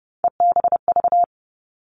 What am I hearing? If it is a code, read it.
Morse code, heard as a keyed tone, about 30 wpm, 700 Hz. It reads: EBV
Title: All Cut Numbers - 3-Digit 30wpm